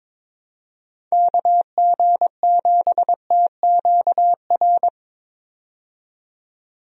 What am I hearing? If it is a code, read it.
Morse code, heard as a keyed tone, about 22 wpm, 700 Hz. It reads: KG7TQR
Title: USA Call Signs - 6 Characters - 22wpm